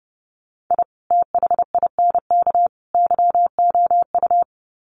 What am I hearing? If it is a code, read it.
Morse code, heard as a keyed tone, about 30 wpm, 700 Hz. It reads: I THINK YOU
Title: Sets of 3 Words 30wpm